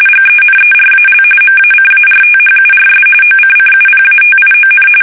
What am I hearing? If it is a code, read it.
RTTY, about 45.45 baud. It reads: GENERAL MACARTHURS OCCUPATION
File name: fsk850 / battleshipnj